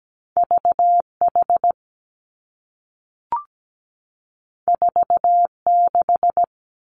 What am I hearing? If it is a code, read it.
Morse code, heard as a keyed tone, about 17 wpm, 700 Hz. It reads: V H 4 6
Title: Two Challenging Letter-Number Combinations 17wpm